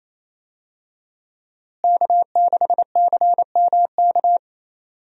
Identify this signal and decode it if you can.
Morse code, heard as a keyed tone, about 28 wpm, 700 Hz. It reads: K6CMK
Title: USA Call Signs - 5 Characters - Repeated 3x - 28wpm